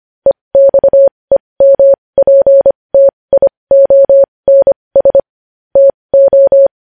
Morse code, heard as a keyed tone, about 25 wpm, spaced as for 20 wpm, 550 Hz. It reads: EXEMPTIONS TO
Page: news-headlines-in-morse-code-at-20-wpm